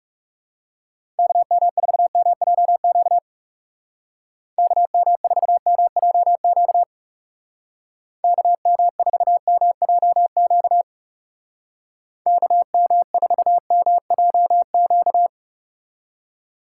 Morse code, heard as a keyed tone, about 38 wpm, 700 Hz. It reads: KM4MJQ KM4MJQ KM4MJQ KM4MJQ